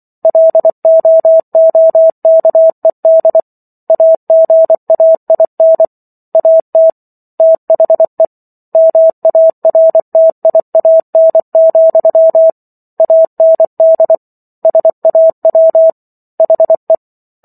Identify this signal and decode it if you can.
Morse code, heard as a keyed tone, about 24 wpm, 650 Hz. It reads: LOOKED AGAIN AT THE MARTIAN, AND SAW HE